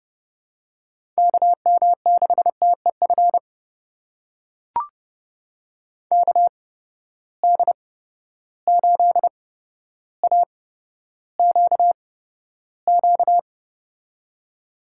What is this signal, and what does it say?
Morse code, heard as a keyed tone, about 32 wpm, spaced as for 13 wpm, 700 Hz. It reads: KM6TEF KD8AQQ
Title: USA Call Signs - 6 Characters - 8x Character Spacing - 30wpm